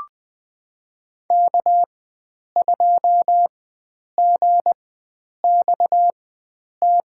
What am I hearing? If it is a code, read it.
Morse code, heard as a keyed tone, about 20 wpm, spaced as for 9 wpm, 700 Hz. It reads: K2GXT
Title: USA Call Signs - 5 Characters - 4x Character Spacing - 20wpm